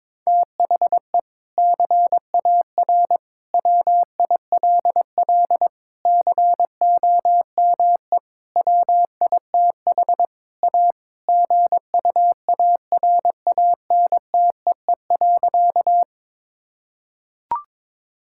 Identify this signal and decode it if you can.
Morse code, heard as a keyed tone, about 22 wpm, 700 Hz. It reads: THE CAR WILL COME WITH A GUARANTEE.
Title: Sentences from Top 1600 Words 22wpm